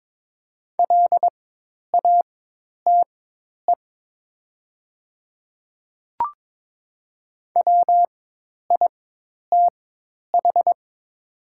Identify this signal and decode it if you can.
Morse code, heard as a keyed tone, about 22 wpm, spaced as for 10 wpm, 700 Hz. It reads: LATE WITH
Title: 4x Character Spacing - Top 300 Words - 22wpm